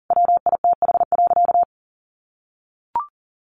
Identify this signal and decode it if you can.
Morse code, heard as a keyed tone, about 40 wpm, 700 Hz. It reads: WITH.
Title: Sentences from Top 1500 Words 40wpm